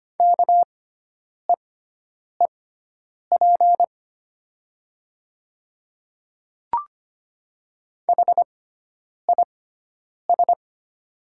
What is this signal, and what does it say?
Morse code, heard as a keyed tone, about 25 wpm, spaced as for 9 wpm, 700 Hz. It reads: KEEP HIS
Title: 6x Character Spacing - Top 300 Words - 25wpm